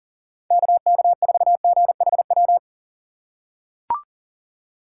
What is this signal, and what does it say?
Morse code, heard as a keyed tone, about 39 wpm, 700 Hz. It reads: KK4GHW
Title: USA Call Signs - 6 Characters - 6x Character Spacing - 40wpm